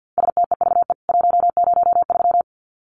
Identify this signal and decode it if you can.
Morse code, heard as a keyed tone, about 50 wpm, 700 Hz. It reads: STEVE 103
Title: CWT Exchange 50wpm